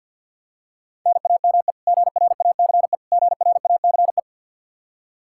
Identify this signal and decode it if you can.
Morse code, heard as a keyed tone, about 47 wpm, 700 Hz. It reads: NAME GRACE GRACE
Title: Practice QSO - Exchange 1 - Beginning - 50wpm